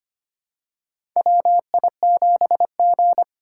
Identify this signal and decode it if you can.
Morse code, heard as a keyed tone, about 25 wpm, 700 Hz. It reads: WI7G
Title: USA Call Signs - 4 Characters - 6x Character Spacing - 25wpm